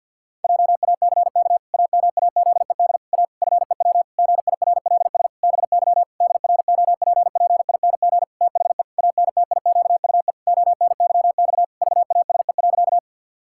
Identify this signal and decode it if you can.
Morse code, heard as a keyed tone, about 50 wpm, 700 Hz. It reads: JACK AMAZED A FEW GIRLS BY DROPPING THE ANTIQUE ONYX VASE.
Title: Pangram Challenge v2 - 50wpm